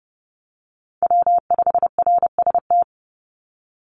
Morse code, heard as a keyed tone, about 30 wpm, 700 Hz. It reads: W5RST
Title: NA SOTA Chaser Callsigns 30wpm